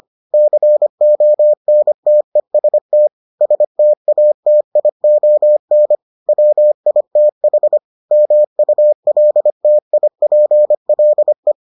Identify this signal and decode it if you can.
Morse code, heard as a keyed tone, about 25 wpm, 600 Hz. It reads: CONTEST STATION WITH MULTIPLE